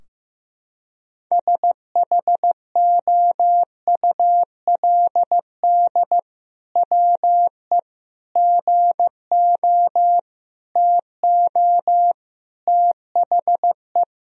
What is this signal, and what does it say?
Morse code, heard as a keyed tone, about 15 wpm, 700 Hz. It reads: SHOULD WE GO TO THE
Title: Sentences from Top 1000 Words - Review Part 3 - 15wpm